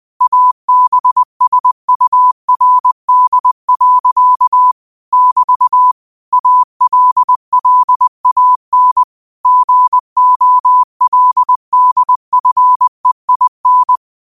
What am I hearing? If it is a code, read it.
Morse code, heard as a keyed tone, about 20 wpm, 1000 Hz. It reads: ABSURD. <BT> ALLAN GOLDFEIN
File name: qotdfiles / 20wpm